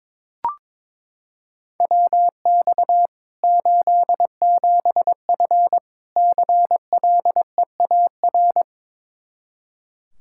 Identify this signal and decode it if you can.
Morse code, heard as a keyed tone, about 22 wpm, 700 Hz. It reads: WX 87F CLEAR
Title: Practice QSO - Exchange 2 - Info - 22wpm